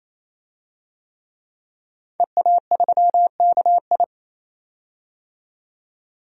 Morse code, heard as a keyed tone, about 28 wpm, 700 Hz. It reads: EA3KI
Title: DXCC-100 Call Signs - 5 Characters - 28wpm